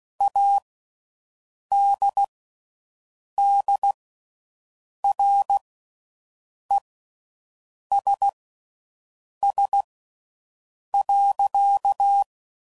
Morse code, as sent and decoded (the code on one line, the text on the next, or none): .- -.. -.. .-. ... ... .-.-.-
ADDRESS.